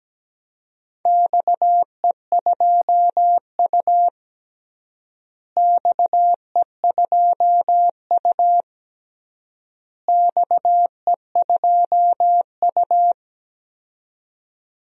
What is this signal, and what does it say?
Morse code, heard as a keyed tone, about 17 wpm, 700 Hz. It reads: XE2U XE2U XE2U
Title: DXCC-100 Call Signs - 4 Characters - Repeated 3x - 17wpm